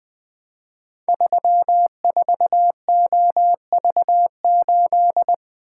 Morse code, heard as a keyed tone, about 20 wpm, 700 Hz. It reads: 34OV8
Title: Code Groups - 5 Letter:Numbers - 20wpm